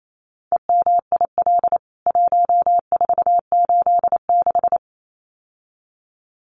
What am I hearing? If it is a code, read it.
Morse code, heard as a keyed tone, about 28 wpm, 700 Hz. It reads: EMIL 1486
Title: CWT Exchange 28wpm